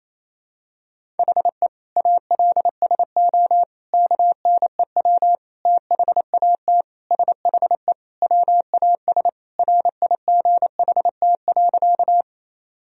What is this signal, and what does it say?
Morse code, heard as a keyed tone, about 28 wpm, 700 Hz. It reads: HE ALSO KNEW THAT SHE WAS RIGHT.
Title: Sentences from Top 1000 Words - Review Part 1 - 28wpm